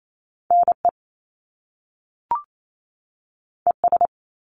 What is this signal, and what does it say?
Morse code, heard as a keyed tone, about 28 wpm, 700 Hz. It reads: NE ES
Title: Top Two Letter Combinations in Words 28wpm